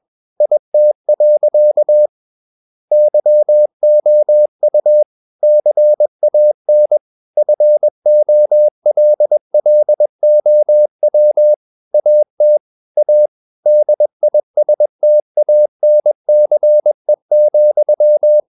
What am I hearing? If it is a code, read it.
Morse code, heard as a keyed tone, about 21 wpm, 600 Hz. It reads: IT. YOU CAN FOLLOW AT A DISTANCE,
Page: CW med Gnister